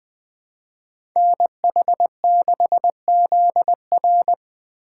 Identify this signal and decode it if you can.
Morse code, heard as a keyed tone, about 20 wpm, 700 Hz. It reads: NH6ZR
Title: USA Call Signs - 5 Characters - 3x Character Spacing - 20wpm